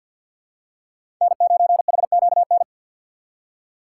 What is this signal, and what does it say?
Morse code, heard as a keyed tone, about 51 wpm, 700 Hz. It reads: N9HQN
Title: USA Call Signs - 5 Characters - 3x Character Spacing - 50wpm